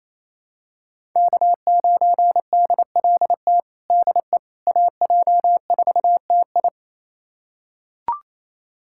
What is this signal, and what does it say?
Morse code, heard as a keyed tone, about 28 wpm, 700 Hz. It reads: K9DLT DE AJ4TI
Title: USA Callsign DE Callsign - 5 Characters - 28wpm